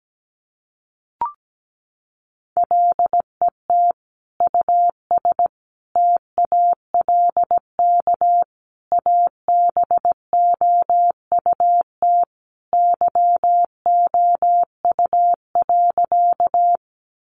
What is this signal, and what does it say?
Morse code, heard as a keyed tone, about 17 wpm, 700 Hz. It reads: LET US TALK ABOUT YOU.
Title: Sentences from Top 400 Words 17wpm